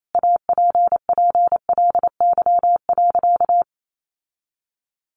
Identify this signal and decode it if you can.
Morse code, heard as a keyed tone, about 28 wpm, 700 Hz. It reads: APPLY.
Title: Sentences from Top 1400 Words 28wpm